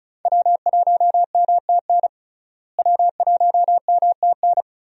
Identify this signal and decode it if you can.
Morse code, heard as a keyed tone, about 36 wpm, 700 Hz. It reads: W1MTN W1MTN
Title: USA Call Signs - 5 Characters - Repeated 3x - 35wpm